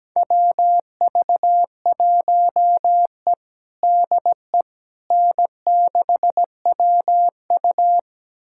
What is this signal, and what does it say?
Morse code, heard as a keyed tone, about 17 wpm, 700 Hz. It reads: WV1E DE N6WU
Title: USA Callsign DE Callsign - 4 Characters - 17wpm